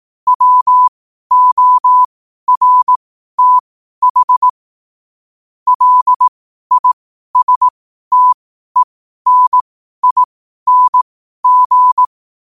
Morse code, heard as a keyed tone, about 18 wpm, spaced as for 12 wpm, 1000 Hz. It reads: WORTH LISTENING